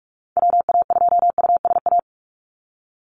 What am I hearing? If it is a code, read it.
Morse code, heard as a keyed tone, about 46 wpm, 700 Hz. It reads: WA2USA
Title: NA SOTA Activator Callsigns 45wpm